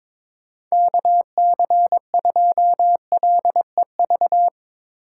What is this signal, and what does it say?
Morse code, heard as a keyed tone, about 21 wpm, 700 Hz. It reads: KC2LEV
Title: USA Call Signs - 6 Characters - 4x Character Spacing - 22wpm